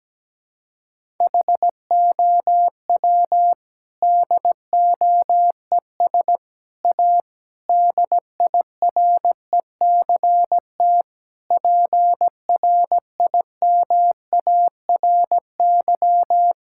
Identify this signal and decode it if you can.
Morse code, heard as a keyed tone, about 17 wpm, 700 Hz. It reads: HOW DOES A DIRECT PRIMARY
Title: Sentences from Top 1000 Words - Review Part 5 - 17wpm